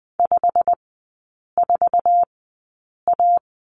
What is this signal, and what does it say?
Morse code, heard as a keyed tone, about 20 wpm, spaced as for 8 wpm, 700 Hz. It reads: 54A
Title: 100 Lesson 17e - Letter u - Three Characters 20wpm